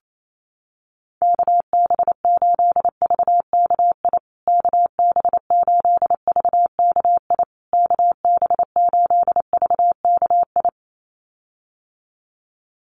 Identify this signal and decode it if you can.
Morse code, heard as a keyed tone, about 28 wpm, 700 Hz. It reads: KB8VKI KB8VKI KB8VKI
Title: USA Call Signs Repeated 3x 28wpm